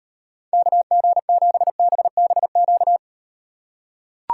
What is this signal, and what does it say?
Morse code, heard as a keyed tone, about 38 wpm, 700 Hz. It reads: KG7BBQ